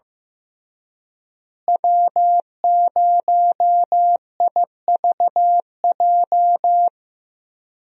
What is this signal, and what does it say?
Morse code, heard as a keyed tone, about 15 wpm, 700 Hz. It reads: W0IVJ
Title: USA Call Signs - 5 Characters - Repeated 2x - 15wpm